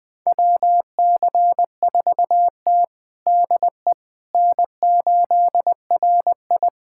Morse code, heard as a keyed tone, about 20 wpm, 700 Hz. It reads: WC4T DE N8RI
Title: USA Callsign DE Callsign - 4 Characters - 20wpm